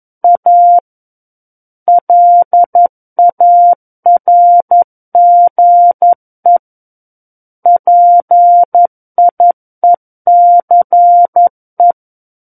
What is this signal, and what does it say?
Morse code, heard as a keyed tone, about 11 wpm, 700 Hz. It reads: A LARGE PIECE
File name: 11 WPM